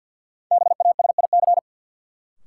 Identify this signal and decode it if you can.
Morse code, heard as a keyed tone, about 47 wpm, 700 Hz. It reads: BASIC
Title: Sentences from Top 1000 Words - Review Part 3 - New Word Drill - 50wpm